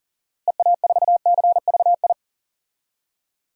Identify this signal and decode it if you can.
Morse code, heard as a keyed tone, about 40 wpm, 700 Hz. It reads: EA4CVI
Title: DXCC-100 Call Signs - 6 Characters - Repeated 2x - 40wpm